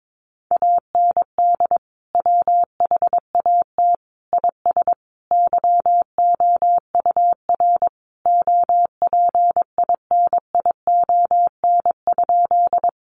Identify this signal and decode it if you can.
Morse code, heard as a keyed tone, about 22 wpm, 700 Hz. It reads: AND WHAT IS YOUR OPINION?
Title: Sentences from Top 1200 Words 22wpm